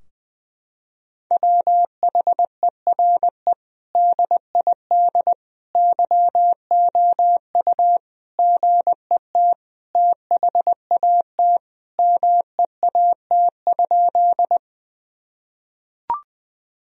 Morse code, heard as a keyed tone, about 20 wpm, 700 Hz. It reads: WHERE DID YOU GET THAT MEAT?
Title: Sentences from Top 1000 Words - Review Part 3 - 20wpm